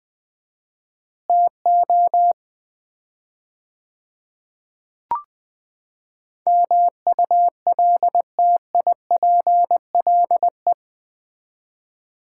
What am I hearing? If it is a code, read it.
Morse code, heard as a keyed tone, about 20 wpm, 700 Hz. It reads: TO MULTIPLE
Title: 4x Word-Spacing - Sets of 5 Words - 20wpm